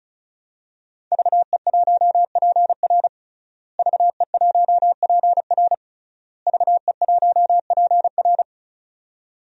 Morse code, heard as a keyed tone, about 36 wpm, 700 Hz. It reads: VE1PR VE1PR VE1PR
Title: DXCC-100 Call Signs - 5 Characters - Repeated 3x - 35wpm